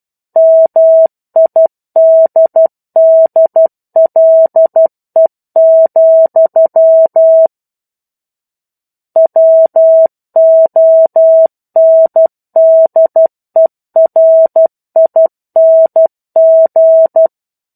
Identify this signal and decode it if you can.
Morse code, heard as a keyed tone, about 12 wpm, 650 Hz. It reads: MIDDLE, WONDERING